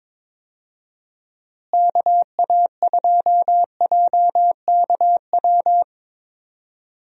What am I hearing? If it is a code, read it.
Morse code, heard as a keyed tone, about 22 wpm, 700 Hz. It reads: KA2JKW
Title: USA Call Signs - 6 Characters - Repeated 3x - 22wpm